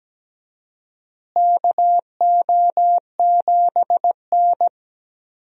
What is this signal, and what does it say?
Morse code, heard as a keyed tone, about 17 wpm, 700 Hz. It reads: KO7N
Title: USA Call Signs - 4 Characters - Repeated 3x - 17wpm